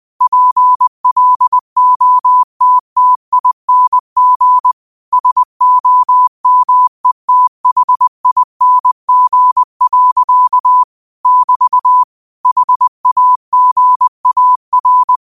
Morse code, heard as a keyed tone, about 20 wpm, 1000 Hz. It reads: PLOTTING SOMETHING. <BT> HAGAR
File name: qotdfiles / 20wpm